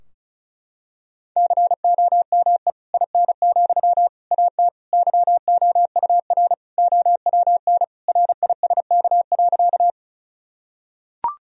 Morse code, heard as a keyed tone, about 34 wpm, 700 Hz. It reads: COME IN, AT YOUR OWN RISK.
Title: Sentences from Top 1000 Words - Review Part 4 - 35wpm